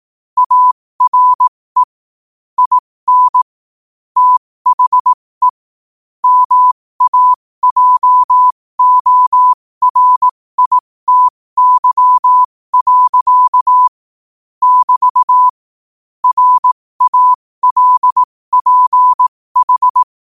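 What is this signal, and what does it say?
Morse code, heard as a keyed tone, about 18 wpm, spaced as for 15 wpm, 1000 Hz. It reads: ARE IN THE MAJORITY. <BT> RALPH
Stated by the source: Morse practice sent at 15 wpm.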